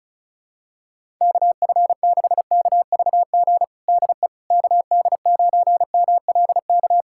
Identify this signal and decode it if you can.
Morse code, heard as a keyed tone, about 34 wpm, 700 Hz. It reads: KF6KVG DE KD9MLK
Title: USA Callsign DE Callsign 35wpm